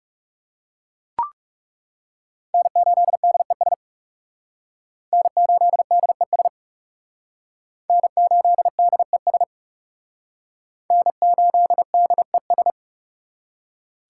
Morse code, heard as a keyed tone, about 38 wpm, 700 Hz. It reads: N8DES N8DES N8DES N8DES